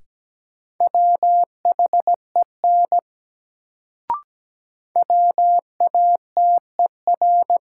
Morse code, heard as a keyed tone, about 17 wpm, 700 Hz. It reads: WHEN WATER
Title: Sentences from Top 100 Words - New Word Drill - 17wpm